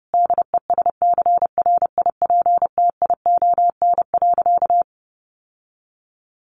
Morse code, heard as a keyed tone, about 30 wpm, 700 Hz. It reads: DESCRIPTION.
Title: Sentences from Top 1500 Words 30wpm